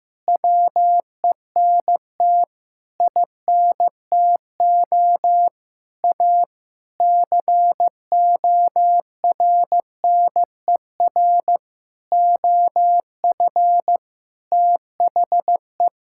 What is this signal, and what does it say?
Morse code, heard as a keyed tone, about 16 wpm, 700 Hz. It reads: WENT INTO A CORNER OF THE
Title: Sentences from Top 1600 Words 15wpm